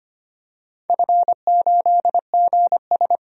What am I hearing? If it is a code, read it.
Morse code, heard as a keyed tone, about 25 wpm, 700 Hz. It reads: F8GS
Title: DXCC-100 Call Signs - 4 Characters - 25wpm